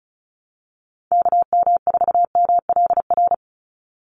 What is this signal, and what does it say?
Morse code, heard as a keyed tone, about 34 wpm, 700 Hz. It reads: KM4MLR
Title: USA Call Signs 35wpm